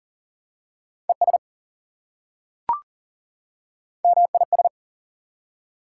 Morse code, heard as a keyed tone, about 40 wpm, 700 Hz. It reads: ES MIS
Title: Prefixes and Suffixes 40wpm